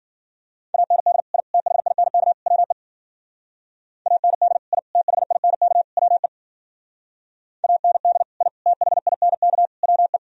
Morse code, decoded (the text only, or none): AND I THINK WE AND I THINK WE AND I THINK WE